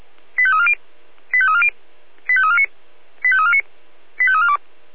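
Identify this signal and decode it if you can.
Five-tone selective call (ZVEI), heard as quick runs of five tones, tones of about 70 ms. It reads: 85340 85340 85349 85349 85342
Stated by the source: ZVEI five-tone selective call received with an SDR.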